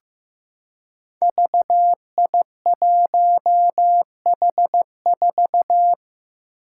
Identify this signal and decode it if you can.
Morse code, heard as a keyed tone, about 15 wpm, 700 Hz. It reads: VI1H4
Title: Five Challenging Letter-Number Combinations 15wpm